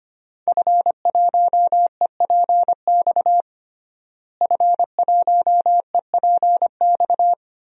Morse code, heard as a keyed tone, about 25 wpm, 700 Hz. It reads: F1EPX F1EPX
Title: DXCC-100 Call Signs - 5 Characters - Repeated 2x - 25wpm